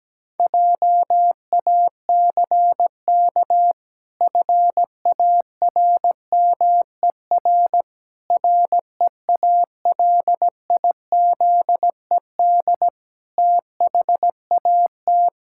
Morse code, heard as a keyed tone, about 17 wpm, 700 Hz. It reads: JACK FARMER REALIZED THAT
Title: Pangram Challenge v2 - 17wpm